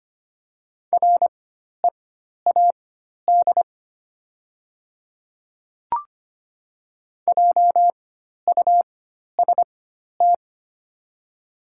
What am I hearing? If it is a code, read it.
Morse code, heard as a keyed tone, about 25 wpm, spaced as for 12 wpm, 700 Hz. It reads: READ JUST
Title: 4x Character Spacing - Top 200 Words - 25wpm